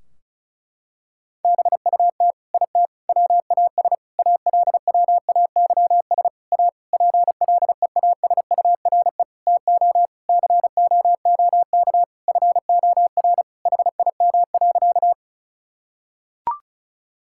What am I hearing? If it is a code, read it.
Morse code, heard as a keyed tone, about 34 wpm, 700 Hz. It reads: BUT IT WAS ALWAYS A PLEASURE TO COOK FOR HIM.
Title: Sentences from Top 1600 Words 35wpm